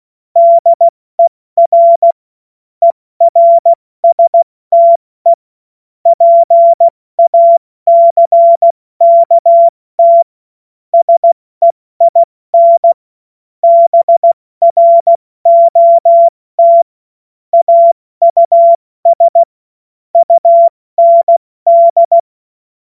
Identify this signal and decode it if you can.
Morse code, heard as a keyed tone, about 16 wpm, 680 Hz. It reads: DER ERSTE PACKT SEIN BROT AUS UND
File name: Morsekurs / Witze / Witz 01. November 2022 70 F80